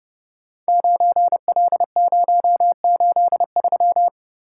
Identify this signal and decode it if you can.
Morse code, heard as a keyed tone, about 30 wpm, 700 Hz. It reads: 9L083
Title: Code Groups - 5 Letter:Numbers - 30wpm